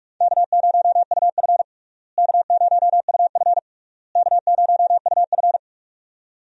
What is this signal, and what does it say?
Morse code, heard as a keyed tone, about 43 wpm, 700 Hz. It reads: K0UF K0UF K0UF
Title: USA Call Signs - 4 Characters - Repeated 3x - 45wpm